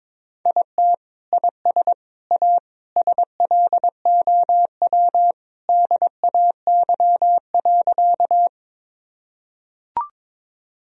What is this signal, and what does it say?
Morse code, heard as a keyed tone, about 22 wpm, 700 Hz. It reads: IT IS A SLOW DAY.
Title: Sentences from Top 1000 Words - Review Part 2 - 22wpm